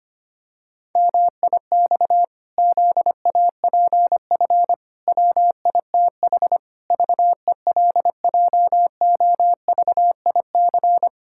Morse code, heard as a keyed tone, about 25 wpm, 700 Hz. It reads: MIX ZAPF WITH VELJOVIC
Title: Pangram Challenge v2 - 25wpm